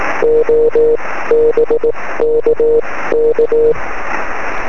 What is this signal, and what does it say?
Morse code, heard as a keyed tone, about 18 wpm, spaced as for 15 wpm, 460 Hz. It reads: O BKK
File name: sm7lcb / 20 23cm / 1605 dk3bu 23cm